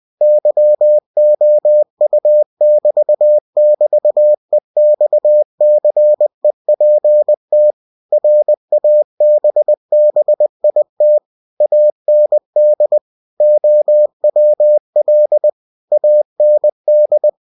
Morse code, heard as a keyed tone, about 20 wpm, 600 Hz. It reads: YOU<BT><BT>EXCEPT RABBIT AND OWL AND
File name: cw-books / winnie-the-pooh / winnie-20wpm-page-0107